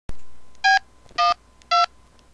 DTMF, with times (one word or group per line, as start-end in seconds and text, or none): 0.6-0.8 B
1.2-1.3 1
1.7-1.9 2